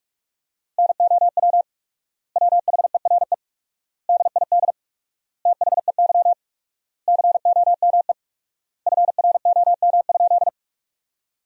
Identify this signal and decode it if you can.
Morse code, heard as a keyed tone, about 46 wpm, 700 Hz. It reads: NOW WHERE DID THEY COME FROM?
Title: 4x Word-Spacing - Sentences from Top 200 Words - 45wpm